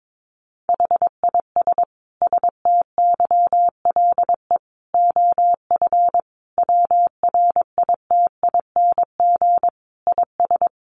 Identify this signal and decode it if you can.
Morse code, heard as a keyed tone, about 22 wpm, 700 Hz. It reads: HIS STYLE OF WRITING IS
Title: Sentences from Top 800 Words 22wpm